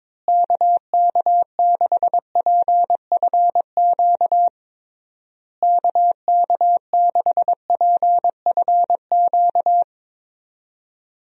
Morse code, heard as a keyed tone, about 22 wpm, 700 Hz. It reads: KK6PFQ KK6PFQ
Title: USA Call Signs - 6 Characters - Repeated 3x - 22wpm